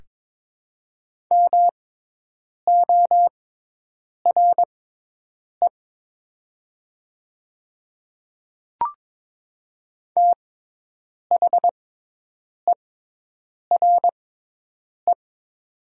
Morse code, heard as a keyed tone, about 22 wpm, spaced as for 8 wpm, 690 Hz. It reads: MORE THERE
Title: 6x Character Spacing - Top 100 Words - 22wpm